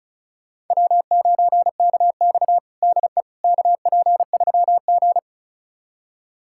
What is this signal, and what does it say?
Morse code, heard as a keyed tone, about 36 wpm, 700 Hz. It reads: W9KX DE KP3G
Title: USA Callsign DE Callsign - 4 Characters - 35wpm